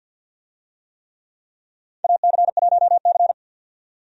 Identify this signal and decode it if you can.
Morse code, heard as a keyed tone, about 51 wpm, 700 Hz. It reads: AC1C
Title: USA Call Signs - 4 Characters - 50wpm